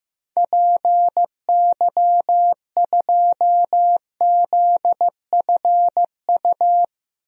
Morse code, heard as a keyed tone, about 15 wpm, 700 Hz. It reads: PY2ZFU
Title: DXCC-100 Call Signs - 6 Characters - 15wpm